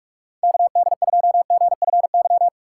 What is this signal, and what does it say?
Morse code, heard as a keyed tone, about 46 wpm, 700 Hz. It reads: KD2GFY
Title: USA Call Signs - 6 Characters - 4x Character Spacing - 45wpm